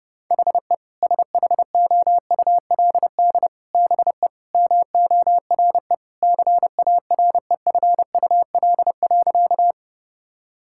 Morse code, heard as a keyed tone, about 32 wpm, 700 Hz. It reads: HE SHOULD BE MORE CAREFUL.
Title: Sentences from Top 1600 Words 30wpm